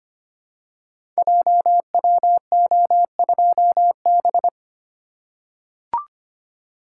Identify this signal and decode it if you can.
Morse code, heard as a keyed tone, about 25 wpm, 700 Hz. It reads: JWO2B
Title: Code Groups - 5 Letter:Numbers - 25wpm